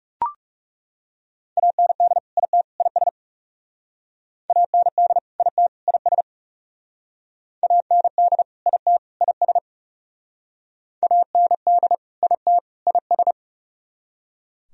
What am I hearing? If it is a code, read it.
Morse code, heard as a keyed tone, about 38 wpm, 700 Hz. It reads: AND IT IS AND IT IS AND IT IS AND IT IS